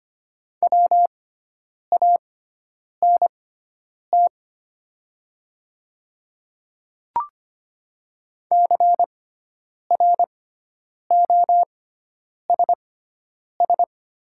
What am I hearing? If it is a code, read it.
Morse code, heard as a keyed tone, about 25 wpm, spaced as for 9 wpm, 700 Hz. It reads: WANT CROSS
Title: 6x Character Spacing - Top 300 Words - 25wpm